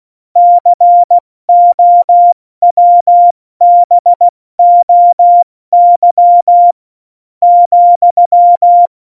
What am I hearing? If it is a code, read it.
Morse code, heard as a keyed tone, about 16 wpm, 700 Hz. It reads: COWBOY ,
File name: Morsekurs / Witze / Witz 06. Oktober 2020, 70 F80